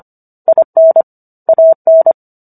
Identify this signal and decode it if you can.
Morse code, heard as a keyed tone, about 25 wpm, 650 Hz. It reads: IN AN